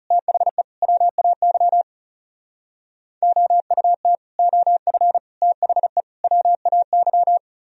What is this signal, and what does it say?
Morse code, heard as a keyed tone, about 38 wpm, 700 Hz. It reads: THE WAY OUT OF THE WAY